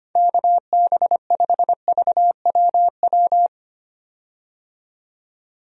Morse code, heard as a keyed tone, about 25 wpm, 700 Hz. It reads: KB5VWW
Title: USA Call Signs - 6 Characters - 25wpm